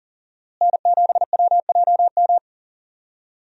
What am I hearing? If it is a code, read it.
Morse code, heard as a keyed tone, about 40 wpm, 690 Hz. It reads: N7WJM
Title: USA Call Signs - 5 Characters - Repeated 3x - 40wpm